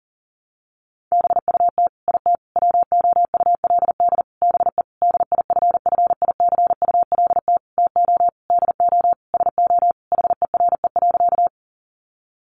Morse code, heard as a keyed tone, about 40 wpm, 700 Hz. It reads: BUT IT WOULD BE DIFFICULT TO DO SO HERE.
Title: Sentences from Top 600 Words 40wpm